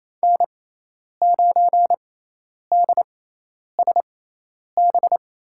Morse code, heard as a keyed tone, about 28 wpm, spaced as for 10 wpm, 700 Hz. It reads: N9DSB
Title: USA Call Signs - 5 Characters - 6x Character Spacing - 28wpm